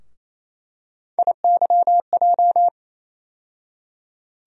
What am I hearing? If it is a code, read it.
Morse code, heard as a keyed tone, about 28 wpm, 700 Hz. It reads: IYJ
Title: USA Call Sign Postfixes - 3 Letters 28wpm